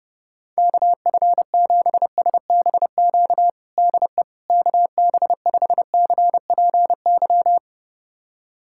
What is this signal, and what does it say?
Morse code, heard as a keyed tone, about 30 wpm, 700 Hz. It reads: KF7SBQ DE KB5CPY
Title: USA Callsign DE Callsign 30wpm